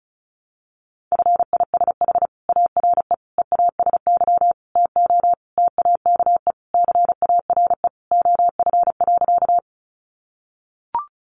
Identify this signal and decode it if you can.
Morse code, heard as a keyed tone, about 34 wpm, 700 Hz. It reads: FISH ARE EASY TO TAKE CARE OF.
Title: Additional Easy Phrases 35wpm